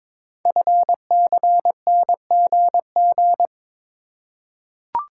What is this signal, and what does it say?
Morse code, heard as a keyed tone, about 22 wpm, 700 Hz. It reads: FCNGG
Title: Code Groups - 5 Letters - 22wpm